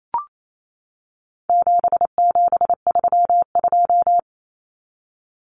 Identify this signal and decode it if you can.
Morse code, heard as a keyed tone, about 28 wpm, 700 Hz. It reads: 7732
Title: Common Cut Numbers - 4-Digit 28wpm